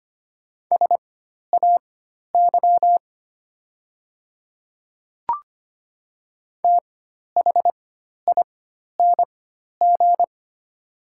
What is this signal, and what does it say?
Morse code, heard as a keyed tone, about 25 wpm, spaced as for 12 wpm, 690 Hz. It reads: SAY THING
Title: 4x Character Spacing - Top 300 Words - 25wpm